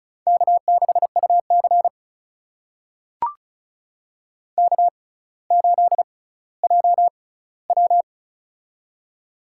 Morse code, heard as a keyed tone, about 36 wpm, spaced as for 18 wpm, 700 Hz. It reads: K6UC K8JW
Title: USA Call Signs - 4 Characters - 6x Character Spacing - 35wpm